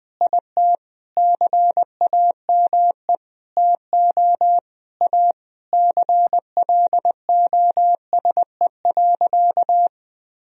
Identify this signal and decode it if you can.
Morse code, heard as a keyed tone, about 21 wpm, 700 Hz. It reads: IT CAME TO A CLOSE.
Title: Sentences from Top 1000 Words - Review Part 1 - 20wpm